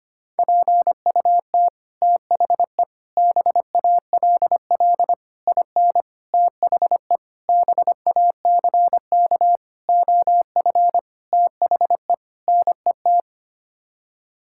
Morse code, heard as a keyed tone, about 25 wpm, 700 Hz. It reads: PUT THE BALL IN THE BACK OF THE NET
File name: Sets of 9 Words - 25wpm